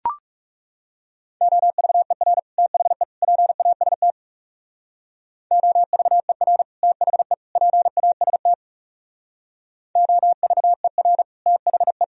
Morse code, heard as a keyed tone, about 35 wpm, 700 Hz. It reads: OVER THE PAST OVER THE PAST OVER THE
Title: Speed-Racing - Sets of 3 Words 30wpm